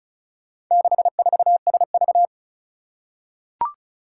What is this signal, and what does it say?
Morse code, heard as a keyed tone, about 36 wpm, 700 Hz. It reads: 64SV
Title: Four Challenging Letter-Number Combinations 35wpm